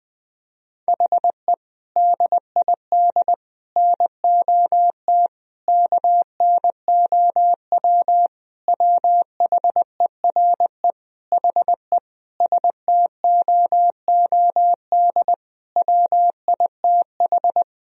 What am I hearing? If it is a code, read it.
Morse code, heard as a keyed tone, about 20 wpm, 700 Hz. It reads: HE DID NOT KNOW WHERE HE STOOD WITH
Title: Sentences from Top 1000 Words - Review Part 2 - 20wpm